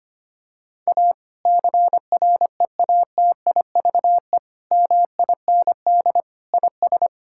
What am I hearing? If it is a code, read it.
Morse code, heard as a keyed tone, about 25 wpm, 700 Hz. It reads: A CREATIVE MIND IS
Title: Sentences from Top 1000 Words - Review Part 5 - 25wpm